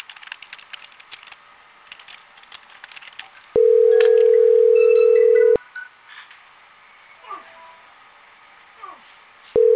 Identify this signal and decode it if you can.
Ringback tone (North America), 440 + 480 Hz.